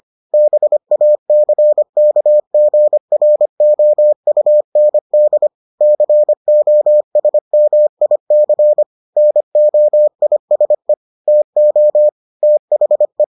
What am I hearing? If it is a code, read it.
Morse code, heard as a keyed tone, about 25 wpm, 600 Hz. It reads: BACKGROUND COSMIC NOISE TO THE